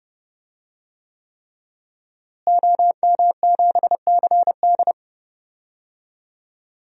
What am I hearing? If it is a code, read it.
Morse code, heard as a keyed tone, about 32 wpm, 700 Hz. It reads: OM7CD
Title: DXCC-100 Call Signs - 5 Characters - Repeated 3x - 30wpm